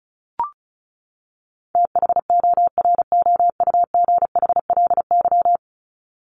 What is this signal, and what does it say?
Morse code, heard as a keyed tone, about 34 wpm, 700 Hz. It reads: THOROUGHLY
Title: LY Suffix - Top 100 Words with Base Words 35wpm